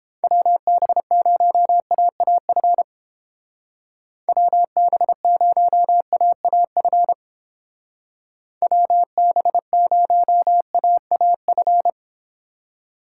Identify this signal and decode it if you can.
Morse code, heard as a keyed tone, about 29 wpm, 700 Hz. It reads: WB0AAF WB0AAF WB0AAF